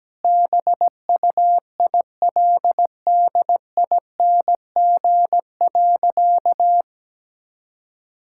Morse code, heard as a keyed tone, about 17 wpm, 700 Hz. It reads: BUILDING.
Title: Sentences from Top 1000 Words - Review Part 5 - 17wpm